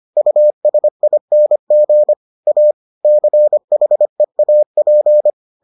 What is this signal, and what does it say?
Morse code, heard as a keyed tone, about 25 wpm, 600 Hz. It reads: USING A CHEAP